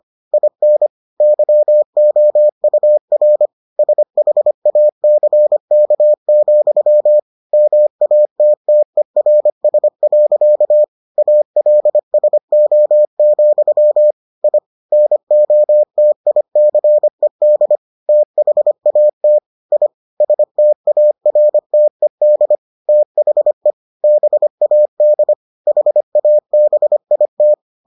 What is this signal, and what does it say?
Morse code, heard as a keyed tone, about 25 wpm, 600 Hz. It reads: IN YOUR SHACK, MATTERS. ALSO, I NOTICED THAT I STARTED THE BAD HABIT